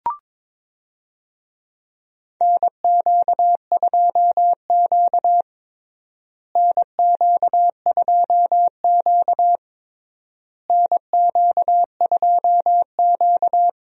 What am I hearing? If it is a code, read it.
Morse code, heard as a keyed tone, about 22 wpm, 700 Hz. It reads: NQ2Q NQ2Q NQ2Q
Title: USA Call Signs - 4 Characters - Repeated 3x - 22wpm